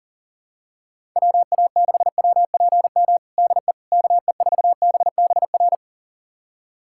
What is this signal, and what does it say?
Morse code, heard as a keyed tone, about 40 wpm, 700 Hz. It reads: WA6WPM DE KE4BBR
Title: USA Callsign DE Callsign - 6 Characters - 40wpm